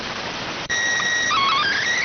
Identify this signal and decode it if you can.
SSTV (slow-scan television), PD 290.